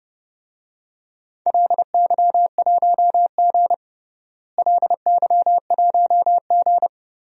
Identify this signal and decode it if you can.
Morse code, heard as a keyed tone, about 30 wpm, 700 Hz. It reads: LY1G LY1G
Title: DXCC-100 Call Signs - 4 Characters - Repeated 3x - 30wpm